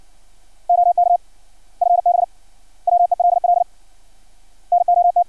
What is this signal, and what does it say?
Morse code, heard as a keyed tone, about 56 wpm, 700 Hz. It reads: OK FB WELL NOT
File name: archived / CRSnet